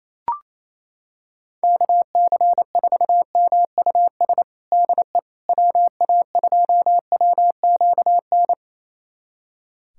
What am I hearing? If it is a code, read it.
Morse code, heard as a keyed tone, about 27 wpm, 700 Hz. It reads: KC4MUS DE WA2WQN
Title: USA Callsign DE Callsign 28wpm